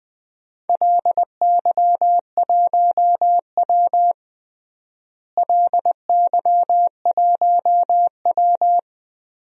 Morse code, heard as a keyed tone, about 20 wpm, 700 Hz. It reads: LY1W LY1W
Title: DXCC-100 Call Signs - 4 Characters - Repeated 3x - 20wpm